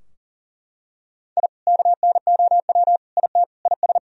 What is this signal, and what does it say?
Morse code, heard as a keyed tone, about 40 wpm, 700 Hz. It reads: I KNOW IT IS
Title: Sentences from Top 1000 Words - Review Part 4 - 40wpm